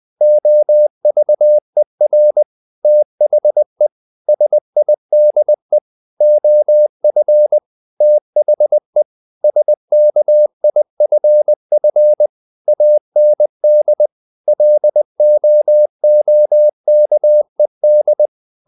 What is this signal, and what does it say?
Morse code, heard as a keyed tone, about 20 wpm, 600 Hz. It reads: OVER THE SIDE OF THE SKIFF AND LOOKED
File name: the old man and the sea / the old man and the sea part 1 20wpm